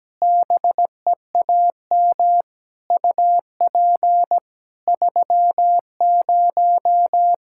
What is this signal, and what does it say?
Morse code, heard as a keyed tone, about 17 wpm, 700 Hz. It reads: BEAM UP 30
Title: Practice QSO - Exchange 2 - Info - 17wpm